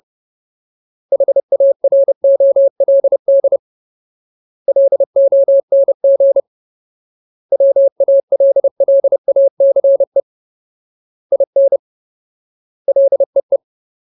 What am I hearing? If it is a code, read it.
Morse code, heard as a keyed tone, about 30 wpm, 550 Hz. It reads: HAROLD LONG WALLACE IN LEE
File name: Ultimate Word List / 30WPM Ultimate